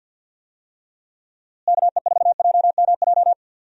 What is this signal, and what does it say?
Morse code, heard as a keyed tone, about 53 wpm, 700 Hz. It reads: KE4JMJ
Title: USA Call Signs - 6 Characters - Repeated 2x - 50wpm